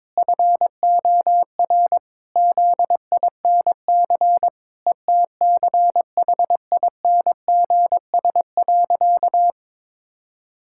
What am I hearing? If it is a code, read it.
Morse code, heard as a keyed tone, about 22 wpm, 700 Hz. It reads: FOR ZINC ETCHINGS.
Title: Pangram Challenge v2 - 22wpm